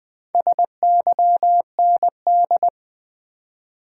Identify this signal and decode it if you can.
Morse code, heard as a keyed tone, about 20 wpm, 700 Hz. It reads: SYND
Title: Build A Word - 4 Letters At A Time - 8-Letter Words - 20wpm